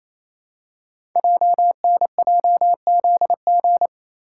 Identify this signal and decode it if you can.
Morse code, heard as a keyed tone, about 27 wpm, 700 Hz. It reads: JNJZG
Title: Code Groups - 5 Letters - 28wpm